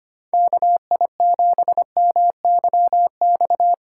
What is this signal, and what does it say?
Morse code, heard as a keyed tone, about 25 wpm, 700 Hz. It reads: KI7MYX
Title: USA Call Signs - 6 Characters - 2x Character Spacing - 25wpm